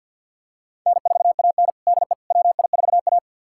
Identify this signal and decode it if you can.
Morse code, heard as a keyed tone, about 50 wpm, 700 Hz. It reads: N4AN DE WI4A
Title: USA Callsign DE Callsign - 4 Characters - 50wpm